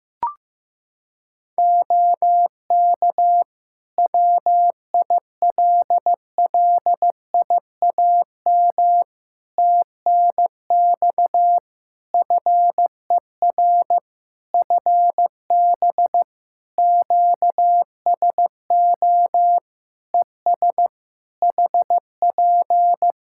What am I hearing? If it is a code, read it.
Morse code, heard as a keyed tone, about 15 wpm, 700 Hz. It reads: OK WILLIAM TNX FER FB QSO ES HP
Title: Practice QSO - Exchange 4 - Ending - 15wpm